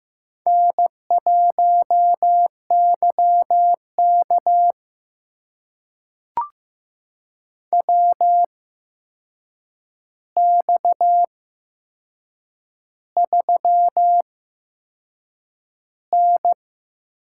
Morse code, heard as a keyed tone, about 15 wpm, spaced as for 6 wpm, 700 Hz. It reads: N1YK WX3N
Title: USA Call Signs - 4 Characters - 8x Character Spacing - 15wpm